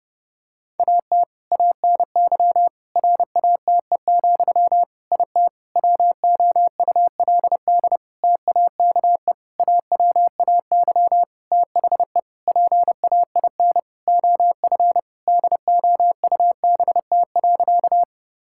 Morse code, heard as a keyed tone, about 32 wpm, 700 Hz. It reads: AT ANY RATE, IT WOULD TAKE AWAY THE PAIN OF DOUBT.
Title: Sentences from Top 1600 Words 30wpm